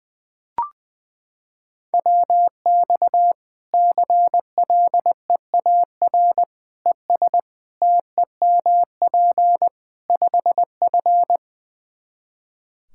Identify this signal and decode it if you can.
Morse code, heard as a keyed tone, about 20 wpm, 700 Hz. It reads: WX CLEAR ES TEMP 5F
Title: Practice QSO - Exchange 2 - Info - 20wpm